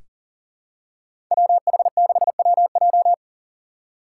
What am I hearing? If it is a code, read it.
Morse code, heard as a keyed tone, about 39 wpm, 700 Hz. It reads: WH6WJ
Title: USA Call Signs - 2x Character Spacing - 40wpm